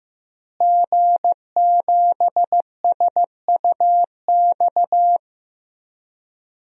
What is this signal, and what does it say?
Morse code, heard as a keyed tone, about 15 wpm, 700 Hz. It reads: G7SUX
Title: DXCC-100 Call Signs - 5 Characters - Repeated 2x - 15wpm